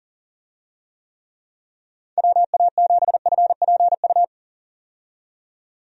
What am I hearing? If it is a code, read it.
Morse code, heard as a keyed tone, about 40 wpm, 700 Hz. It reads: WA7FPU
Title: USA Call Signs - 6 Characters - 40wpm